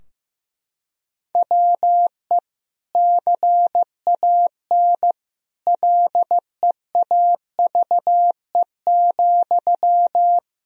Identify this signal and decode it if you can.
Morse code, heard as a keyed tone, about 15 wpm, 700 Hz. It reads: WE CAN LEAVE,
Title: Sentences from Top 1000 Words - Review Part 4 - 15wpm